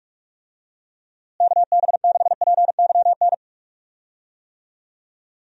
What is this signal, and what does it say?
Morse code, heard as a keyed tone, about 46 wpm, 700 Hz. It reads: KB6PYN
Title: USA Call Signs - 6 Characters - Repeated 2x - 45wpm